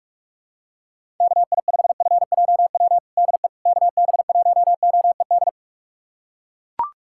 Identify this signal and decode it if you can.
Morse code, heard as a keyed tone, about 42 wpm, 700 Hz. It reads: KI5FJW DE KB1OED
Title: USA Callsign DE Callsign - 6 Characters - 45wpm